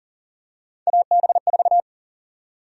Morse code, heard as a keyed tone, about 40 wpm, 700 Hz. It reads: AB4
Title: USA Call Sign Prefixes - Repeated 2x - 40wpm